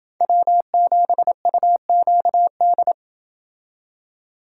Morse code, heard as a keyed tone, about 29 wpm, 700 Hz. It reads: W7UQD